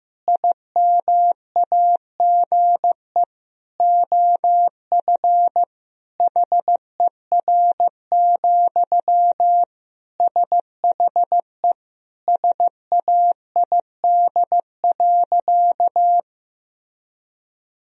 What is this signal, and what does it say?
Morse code, heard as a keyed tone, about 16 wpm, 700 Hz. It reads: IMAGE OF HER, SHE SAID.
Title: Sentences from Top 1000 Words - Review Part 4 - 15wpm